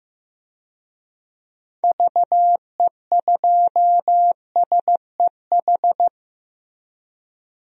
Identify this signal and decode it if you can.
Morse code, heard as a keyed tone, about 16 wpm, 700 Hz. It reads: VE2SEH
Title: DXCC-100 Call Signs - 6 Characters - Repeated 2x - 15wpm